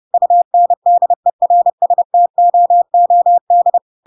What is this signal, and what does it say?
Morse code, heard as a keyed tone, about 30 wpm, 700 Hz. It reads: UNDERSTOOD
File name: Pinocchio / Pinocchio ch8 30 30 ik1hge